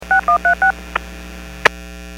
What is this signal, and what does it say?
DTMF, heard as telephone touch tones, tones of about 110 ms. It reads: 6136
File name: WPS / 02591 04-49 WPS 451.25 WPS